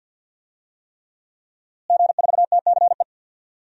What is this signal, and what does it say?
Morse code, heard as a keyed tone, about 46 wpm, 700 Hz. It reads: G4TCE
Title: DXCC-100 Call Signs - 5 Characters - Repeated 3x - 50wpm